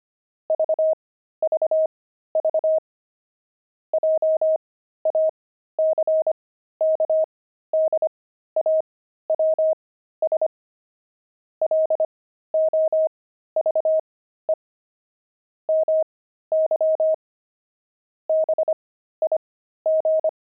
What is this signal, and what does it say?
Morse code, heard as a keyed tone, about 25 wpm, spaced as for 13 wpm, 630 Hz. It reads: VVV JACKDAWS LOVE MY BIG
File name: pangram-fs-13-cs-25-jackdaws